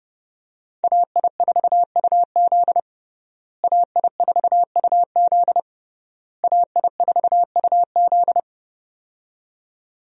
Morse code, heard as a keyed tone, about 30 wpm, 700 Hz. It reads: AI4UZ AI4UZ AI4UZ
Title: USA Call Signs - 5 Characters - Repeated 3x - 30wpm